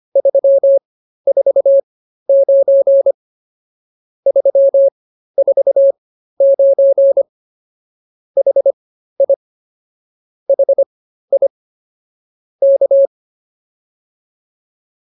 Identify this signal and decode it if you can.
Morse code, heard as a keyed tone, about 24 wpm, spaced as for 13 wpm, 550 Hz. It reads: 349 349 HI HI K